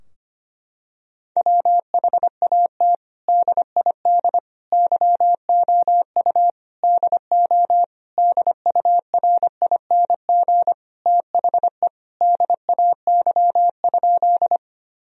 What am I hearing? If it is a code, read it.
Morse code, heard as a keyed tone, about 25 wpm, 700 Hz. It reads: WHAT DID YOU DO DURING THE DAY?
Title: Sentences from Top 1000 Words - Review Part 2 - 25wpm